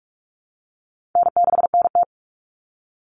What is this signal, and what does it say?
Morse code, heard as a keyed tone, about 44 wpm, 700 Hz. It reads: N6NT
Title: Common Cut Numbers - 4-Digit 45wpm